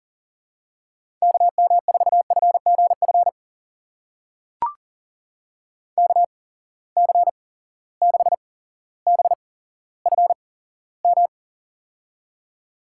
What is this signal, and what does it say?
Morse code, heard as a keyed tone, about 39 wpm, spaced as for 17 wpm, 700 Hz. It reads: KM4FGF KC6BFM
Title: USA Call Signs - 6 Characters - 8x Character Spacing - 40wpm